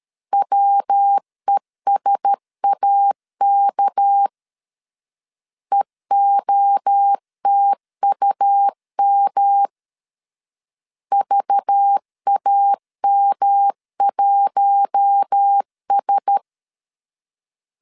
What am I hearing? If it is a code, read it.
Morse code, heard as a keyed tone, about 13 wpm, 800 Hz. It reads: WESAK EOTUM VAM1S
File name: LEZIONI / 6 PKX0 / 6 PK 12wpm 2006